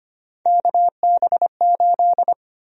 Morse code, heard as a keyed tone, about 25 wpm, 700 Hz. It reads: KB8
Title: USA Call Sign Prefixes - 25wpm